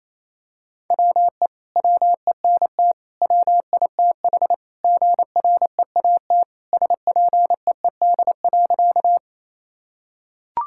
Morse code, heard as a keyed tone, about 28 wpm, 700 Hz. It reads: WE WENT WITH GREAT SPEED.
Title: Sentences from Top 1000 Words - Review Part 5 - 28wpm